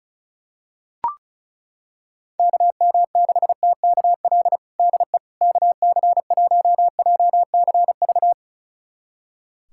Morse code, heard as a keyed tone, about 34 wpm, 700 Hz. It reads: KM6TKL DE KC1JCV
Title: USA Callsign DE Callsign 35wpm